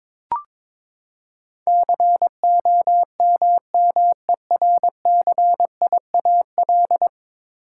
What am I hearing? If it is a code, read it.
Morse code, heard as a keyed tone, about 22 wpm, 700 Hz. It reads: COMMERCIAL
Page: morse-code-ninja-practice-22wpm